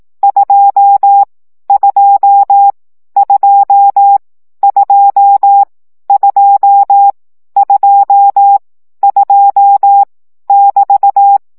Morse code, heard as a keyed tone, about 18 wpm, 800 Hz. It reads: 2 2 2 2 2 2 2 <BT>
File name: CW gyakorlas 2 15 baud